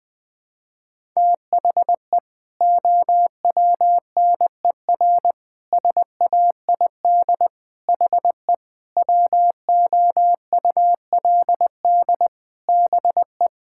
Morse code, heard as a keyed tone, about 20 wpm, 700 Hz. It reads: THE OWNER SAID HE WOULD BE
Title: Sentences from Top 1600 Words 20wpm